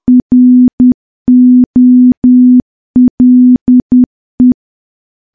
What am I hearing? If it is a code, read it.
Morse code, heard as a keyed tone, about 10 wpm, 250 Hz. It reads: ROLE
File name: morse / morse-training-10WPM